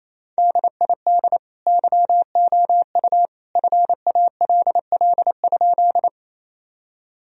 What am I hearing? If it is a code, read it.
Morse code, heard as a keyed tone, about 28 wpm, 700 Hz. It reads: DID YOU FALL?
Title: Sentences from Top 1000 Words - Review Part 2 - 28wpm